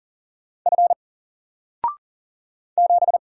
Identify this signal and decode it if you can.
Morse code, heard as a keyed tone, about 40 wpm, 700 Hz. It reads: F 7
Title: Single Character (Letter:Number:Punctuation:Prosign) Alt 40wpm